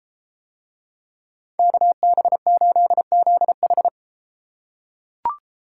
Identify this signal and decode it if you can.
Morse code, heard as a keyed tone, about 31 wpm, 700 Hz. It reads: KB8ZH